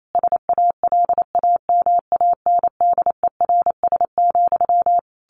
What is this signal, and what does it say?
Morse code, heard as a keyed tone, about 28 wpm, 700 Hz. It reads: SALAMANDERS,
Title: A Sound Of Thunder 28wpm